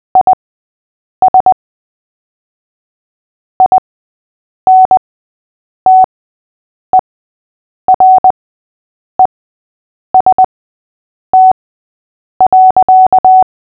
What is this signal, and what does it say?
Morse code, heard as a keyed tone, about 20 wpm, spaced as for 8 wpm, 750 Hz. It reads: IS INTEREST.
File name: cw / fists / Short-Story-8-8wpm